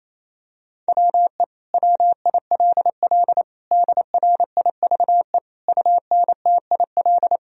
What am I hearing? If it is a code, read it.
Morse code, heard as a keyed tone, about 28 wpm, 700 Hz. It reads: WE WILL DRIVE UNTIL
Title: Sentences from Top 1000 Words - Review Part 2 - 28wpm